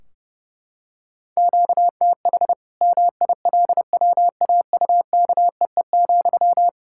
Morse code, heard as a keyed tone, about 30 wpm, 700 Hz. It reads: QTH MILWAUKEE,
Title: K1USN SST QSO Practice 30wpm